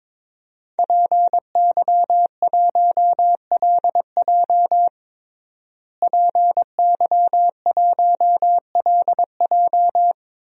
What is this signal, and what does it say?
Morse code, heard as a keyed tone, about 22 wpm, 700 Hz. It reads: PY1LJ PY1LJ
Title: DXCC-100 Call Signs - 5 Characters - Repeated 2x - 22wpm